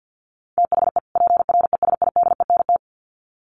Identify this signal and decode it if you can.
Morse code, heard as a keyed tone, about 50 wpm, 700 Hz. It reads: THE PRESIDENT
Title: Sets of 2 Words 50wpm